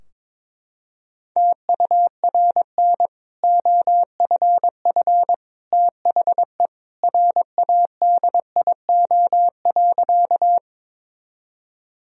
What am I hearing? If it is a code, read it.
Morse code, heard as a keyed tone, about 22 wpm, 700 Hz. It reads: TURN OFF THE RADIO.
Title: Sentences from Top 1000 Words - Review Part 3 - 22wpm